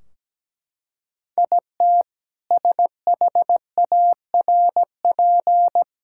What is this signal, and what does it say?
Morse code, heard as a keyed tone, about 17 wpm, 700 Hz. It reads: IT SHARP
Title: Sentences from Top 1600 Words 17wpm